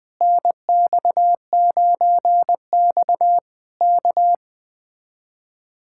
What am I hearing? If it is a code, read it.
Morse code, heard as a keyed tone, about 20 wpm, 700 Hz. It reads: NX9X K
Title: Speed-Racing Common QSO Phrases - 17wpm